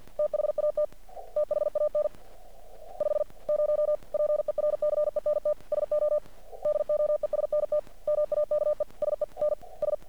Morse code, heard as a keyed tone, about 50 wpm, 600 Hz. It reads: THAT THAN 50 PERCENT SO DOESNT MAKE SENS
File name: archived / CRSnet